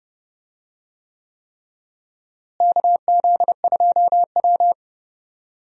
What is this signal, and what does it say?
Morse code, heard as a keyed tone, about 30 wpm, 700 Hz. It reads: KZ2W
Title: USA Call Signs - 4 Characters - Repeated 3x - 30wpm